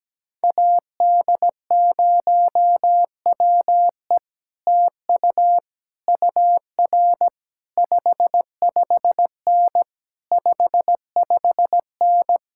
Morse code, heard as a keyed tone, about 17 wpm, 700 Hz. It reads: AD0WE TU UR 55N 55N
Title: Speed-Racing Common QSO Phrases - 17wpm